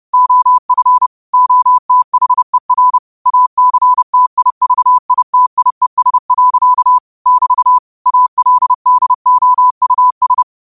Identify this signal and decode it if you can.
Morse code, heard as a keyed tone, about 30 wpm, 1000 Hz. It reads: OF OTHER ACTIVITIES. <BT> ALDOUS
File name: qotdfiles / 30wpm